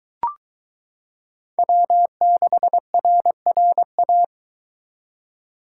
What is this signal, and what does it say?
Morse code, heard as a keyed tone, about 23 wpm, 700 Hz. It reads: W6RRA